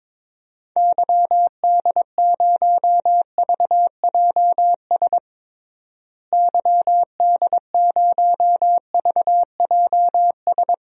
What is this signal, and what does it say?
Morse code, heard as a keyed tone, about 22 wpm, 700 Hz. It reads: YD0VJS YD0VJS
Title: DXCC-100 Call Signs - 6 Characters - Repeated 2x - 22wpm